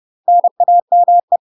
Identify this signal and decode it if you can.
Morse code, heard as a keyed tone, about 30 wpm, 700 Hz. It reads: NAME
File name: UK MASTER QSOs TRAINING / 30WPM / UK MASTER QSOs 0041